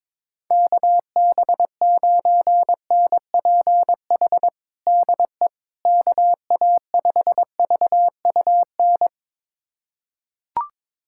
Morse code, heard as a keyed tone, about 22 wpm, 700 Hz. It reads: KB9NPH DE KA5VUN
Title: USA Callsign DE Callsign 22wpm